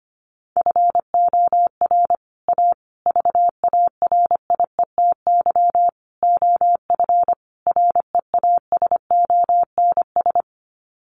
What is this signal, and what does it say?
Morse code, heard as a keyed tone, about 25 wpm, 700 Hz. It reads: FOR A VARIETY OF REASONS
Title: Sets of 5 Words Encore 25wpm